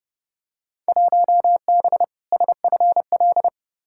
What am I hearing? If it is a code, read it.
Morse code, heard as a keyed tone, about 30 wpm, 700 Hz. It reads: 1B SFL
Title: Field Day Exchanges Encore - 30wpm